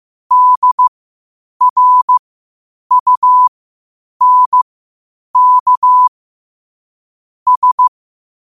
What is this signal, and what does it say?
Morse code, heard as a keyed tone, about 15 wpm, spaced as for 8 wpm, 1000 Hz. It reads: DRUNK S
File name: qotdfiles / 8wpm